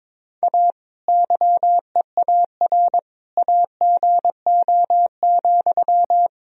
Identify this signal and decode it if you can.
Morse code, heard as a keyed tone, about 22 wpm, 700 Hz. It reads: A YEAR AGO,
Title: Sentences from Top 1000 Words - Review Part 3 - 22wpm